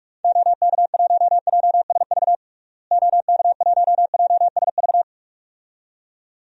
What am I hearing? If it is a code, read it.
Morse code, heard as a keyed tone, about 47 wpm, 700 Hz. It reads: OK1JSV OK1JSV
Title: DXCC-100 Call Signs - 6 Characters - Repeated 2x - 45wpm